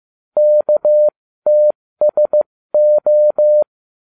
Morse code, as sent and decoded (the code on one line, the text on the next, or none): -.- - ... ---
KTSO